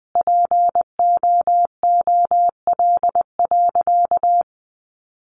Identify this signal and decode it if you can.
Morse code, heard as a keyed tone, about 20 wpm, 700 Hz. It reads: POOL.
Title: Easy Phrases 20wpm